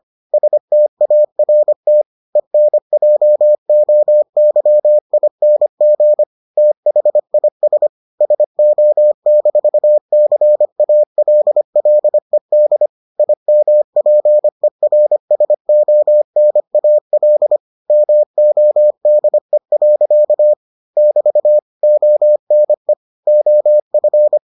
Morse code, heard as a keyed tone, about 25 wpm, 600 Hz. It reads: START ENJOYING THIS SO-CALLED IMPERSONAL MODE. <BT> ONE OF